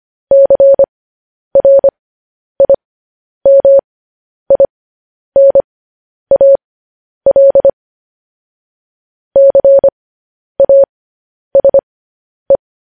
Morse code, heard as a keyed tone, about 25 wpm, spaced as for 10 wpm, 550 Hz. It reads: CRIMINAL CASE